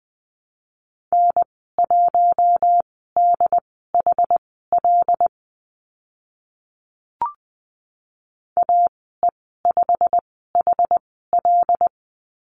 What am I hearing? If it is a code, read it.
Morse code, heard as a keyed tone, about 19 wpm, spaced as for 14 wpm, 700 Hz. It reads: N1DHL AE5HL
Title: 088 Lesson 15g - Number 5 - Call Signs 20wpm